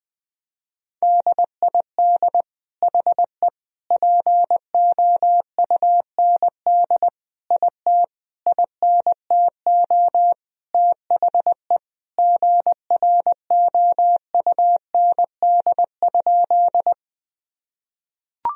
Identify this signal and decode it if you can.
Morse code, heard as a keyed tone, about 20 wpm, 700 Hz. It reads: DID HE POUND IT INTO THE GROUND?
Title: Sentences from Top 1000 Words - Review Part 2 - 20wpm